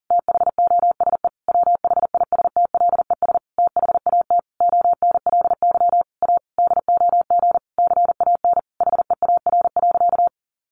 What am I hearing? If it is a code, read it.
Morse code, heard as a keyed tone, about 40 wpm, 700 Hz. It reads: THOSE WHISTLES THAT ONLY A DOG CAN HEAR.
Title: A Sound Of Thunder 40wpm